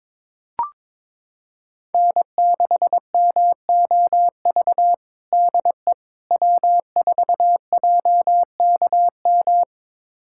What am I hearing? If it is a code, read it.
Morse code, heard as a keyed tone, about 22 wpm, 700 Hz. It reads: N6MOV DE W4JKM
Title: USA Callsign DE Callsign - 5 Characters - 22wpm